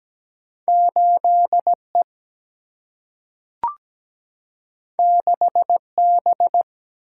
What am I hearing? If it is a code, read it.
Morse code, heard as a keyed tone, about 17 wpm, 700 Hz. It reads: 8E 6B
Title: Single-Letter Single-Number 17wpm